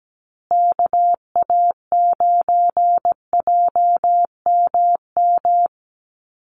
Morse code, heard as a keyed tone, about 17 wpm, 700 Hz. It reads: KA9JMM
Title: USA Call Signs Repeated 2x 17wpm